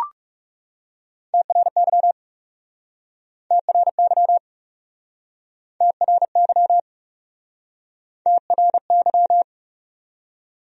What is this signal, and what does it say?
Morse code, heard as a keyed tone, about 40 wpm, 700 Hz. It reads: TRY TRY TRY TRY